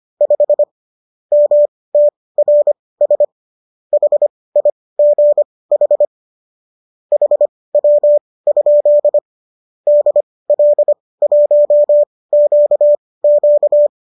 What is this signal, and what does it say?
Morse code, heard as a keyed tone, about 25 wpm, spaced as for 18 wpm, 590 Hz. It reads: 5 MTRS HIGH HW? DL1QQ